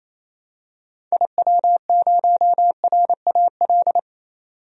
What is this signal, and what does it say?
Morse code, heard as a keyed tone, about 28 wpm, 700 Hz. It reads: IW0RAL
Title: DXCC-100 Call Signs - 6 Characters - 28wpm